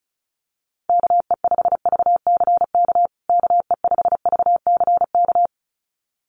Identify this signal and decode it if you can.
Morse code, heard as a keyed tone, about 34 wpm, 700 Hz. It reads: KE5VCK KE5VCK
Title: USA Call Signs Repeated 2x 35wpm